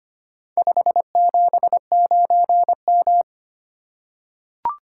Morse code, heard as a keyed tone, about 25 wpm, 700 Hz. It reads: 579M
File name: practice / 25wpm / Four Letter-Number Combinations - 25wpm